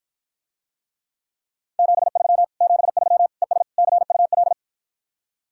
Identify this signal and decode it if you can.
Morse code, heard as a keyed tone, about 54 wpm, 700 Hz. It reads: 73 73 ES CUL